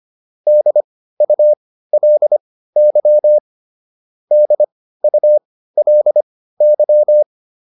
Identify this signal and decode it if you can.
Morse code, heard as a keyed tone, about 25 wpm, spaced as for 15 wpm, 600 Hz. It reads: DULY DULY